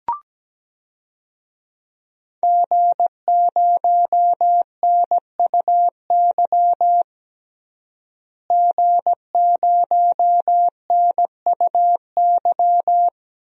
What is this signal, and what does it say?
Morse code, heard as a keyed tone, about 17 wpm, 700 Hz. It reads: G0NUY G0NUY
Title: DXCC-100 Call Signs - 5 Characters - Repeated 2x - 17wpm